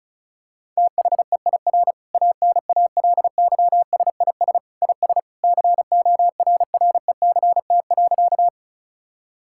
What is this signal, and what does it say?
Morse code, heard as a keyed tone, about 36 wpm, 700 Hz. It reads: THEIR ANALYSIS IS CORRECT.
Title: Sentences from Top 1000 Words - Review Part 5 - 35wpm